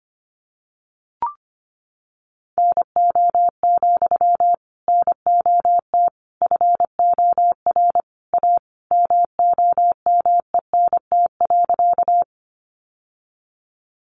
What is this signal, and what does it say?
Morse code, heard as a keyed tone, about 25 wpm, 700 Hz. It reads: NO, NOT FOR A MOMENT.
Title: Sentences from Top 1000 Words 25wpm